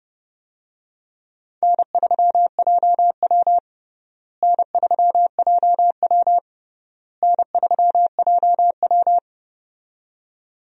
Morse code, heard as a keyed tone, about 30 wpm, 700 Hz. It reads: N3JW N3JW N3JW
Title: USA Call Signs - 4 Characters - Repeated 3x - 30wpm